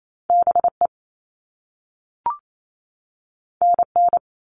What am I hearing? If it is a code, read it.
Morse code, heard as a keyed tone, about 28 wpm, 700 Hz. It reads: BE NN
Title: All Cut Numbers - 2-Digit 28wpm